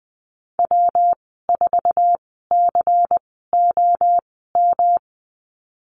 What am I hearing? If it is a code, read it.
Morse code, heard as a keyed tone, about 20 wpm, spaced as for 15 wpm, 700 Hz. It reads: W4COM
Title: 179 Lesson 29g - Slash - Call Signs 20wpm